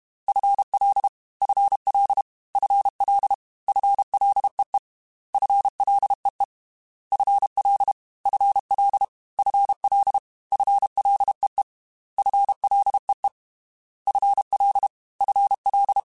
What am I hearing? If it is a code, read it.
Morse code, heard as a keyed tone, about 32 wpm, 800 Hz. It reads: FL FL FL FLEE FLEE FL FL FL FLEE FLEE FL FL
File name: FL / FL mp3s / FL Prefixes 3x2 Twice @30